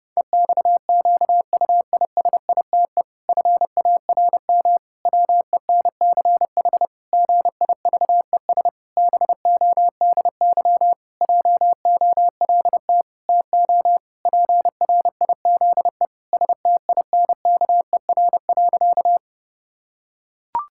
Morse code, heard as a keyed tone, about 32 wpm, 700 Hz. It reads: EXQUISITE FARM WENCH GIVES BODY JOLT TO PRIZE STINKER.
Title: Pangram Challenge v2 - 30wpm